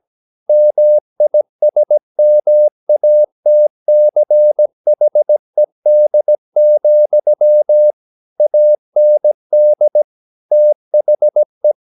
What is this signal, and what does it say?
Morse code, heard as a keyed tone, about 17 wpm, 600 Hz. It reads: MISMATCHED, AND THE
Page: Romanen Never af Ken Follet på engelsk.